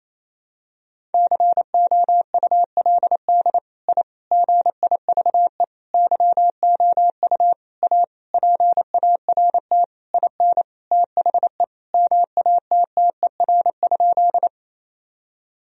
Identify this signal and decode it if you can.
Morse code, heard as a keyed tone, about 28 wpm, 700 Hz. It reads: COULD I GIVE YOU A PART IN THE MATTER?
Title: Sentences from Top 1000 Words - Review Part 1 - 28wpm